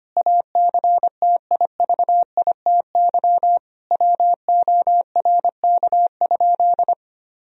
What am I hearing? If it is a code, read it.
Morse code, heard as a keyed tone, about 25 wpm, 700 Hz. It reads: ACTIVITY WORK?
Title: Sentences from Top 1000 Words - Review Part 4 - 25wpm